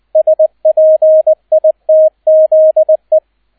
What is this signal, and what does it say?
Morse code, heard as a keyed tone, about 20 wpm, 630 Hz, with radio noise. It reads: SPITZE